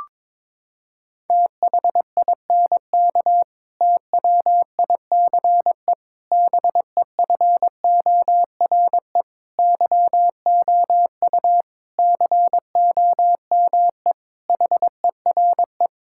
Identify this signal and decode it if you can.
Morse code, heard as a keyed tone, about 22 wpm, 700 Hz. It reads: THINK TWICE BEFORE YOU COME HERE
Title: Sentences from Top 1600 Words 22wpm